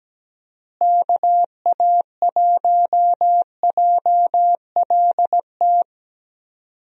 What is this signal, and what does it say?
Morse code, heard as a keyed tone, about 17 wpm, 700 Hz. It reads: KA1JLT
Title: USA Call Signs - 6 Characters - Repeated 2x - 17wpm